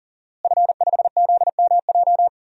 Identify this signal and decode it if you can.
Morse code, heard as a keyed tone, about 40 wpm, 700 Hz. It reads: F5ZMJ
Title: DXCC-100 Call Signs - 5 Characters - Repeated 2x - 40wpm